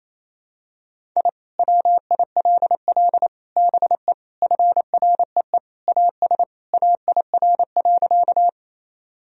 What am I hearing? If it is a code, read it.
Morse code, heard as a keyed tone, about 28 wpm, 700 Hz. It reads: I WILL BE FREE AS AIR.
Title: Sentences from Top 1000 Words - Review Part 1 - 28wpm